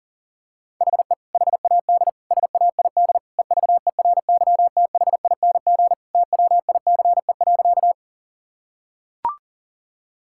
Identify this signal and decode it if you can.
Morse code, heard as a keyed tone, about 39 wpm, 700 Hz. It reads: HE HAD SAID EVERYTHING TWICE.
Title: Sentences from Top 1600 Words 40wpm